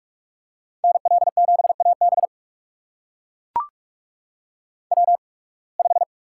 Morse code, heard as a keyed tone, about 43 wpm, spaced as for 28 wpm, 700 Hz. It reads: NL7AB W5
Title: USA Call Signs - 5 Characters - 8x Character Spacing - 45wpm